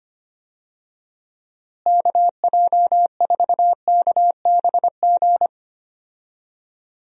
Morse code, heard as a keyed tone, about 25 wpm, 700 Hz. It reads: KJ4KBG
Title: USA Call Signs - 6 Characters - Repeated 2x - 25wpm